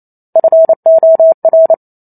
Morse code, heard as a keyed tone, about 28 wpm, 650 Hz. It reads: FOR